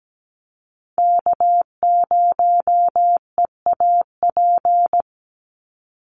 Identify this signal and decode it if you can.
Morse code, heard as a keyed tone, about 17 wpm, 700 Hz. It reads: K0EAP
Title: NA SOTA Chaser Callsigns 17wpm